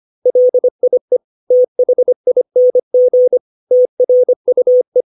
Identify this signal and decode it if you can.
Morse code, heard as a keyed tone, about 25 wpm, 500 Hz. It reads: LIE THING TRUE
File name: 632 words / 632 Words 25WPM